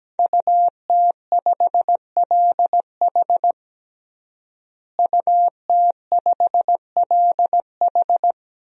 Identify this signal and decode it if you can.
Morse code, heard as a keyed tone, about 17 wpm, 700 Hz. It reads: UT5LH UT5LH
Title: DXCC-100 Call Signs - 5 Characters - Repeated 3x - 17wpm